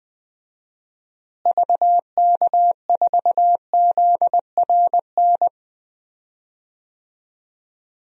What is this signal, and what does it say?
Morse code, heard as a keyed tone, about 21 wpm, 700 Hz. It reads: VK4ZRN
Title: DXCC-100 Call Signs - 6 Characters - Repeated 2x - 20wpm